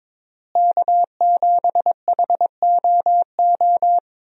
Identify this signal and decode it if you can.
Morse code, heard as a keyed tone, about 21 wpm, 700 Hz. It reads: K7HOO
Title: USA Call Signs - 5 Characters - Repeated 3x - 22wpm